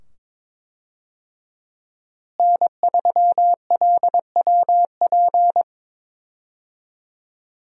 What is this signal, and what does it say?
Morse code, heard as a keyed tone, about 22 wpm, 700 Hz. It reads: N3LWP
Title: USA Call Signs - 5 Characters - Repeated 2x - 22wpm